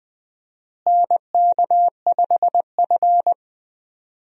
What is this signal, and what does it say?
Morse code, heard as a keyed tone, about 20 wpm, 700 Hz. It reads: NK5F
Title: USA Call Signs - 4 Characters - Repeated 3x - 20wpm